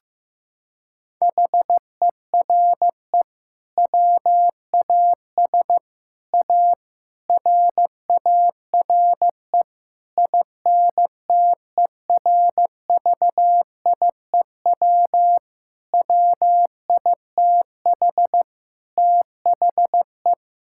Morse code, heard as a keyed tone, about 15 wpm, 700 Hz. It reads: HERE WAS A RARE INTERVIEW WITH THE
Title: Sentences from Top 1600 Words 15wpm